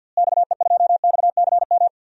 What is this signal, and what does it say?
Morse code, heard as a keyed tone, about 51 wpm, 700 Hz. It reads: XE2XCM
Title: DXCC-100 Call Signs - 6 Characters - Repeated 3x - 50wpm